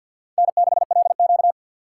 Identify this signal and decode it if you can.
Morse code, heard as a keyed tone, about 53 wpm, 700 Hz. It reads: N6LQ
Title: USA Call Signs - 4 Characters - 2x Character Spacing - 50wpm